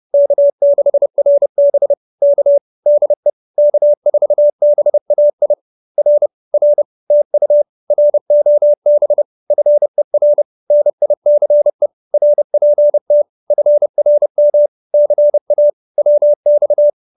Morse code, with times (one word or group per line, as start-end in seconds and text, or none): 0.1-1.9 K6RB
2.2-2.6 K
2.9-3.3 DE
3.6-5.5 K4BAI
6.0-6.3 R
6.5-6.8 R
7.1-7.6 TU
7.9-9.2 ROB
9.5-10.4 FER
10.7-11.9 NICE
12.1-13.2 RPT
13.5-14.7 FRM
14.9-15.7 CA
16.0-16.9 WX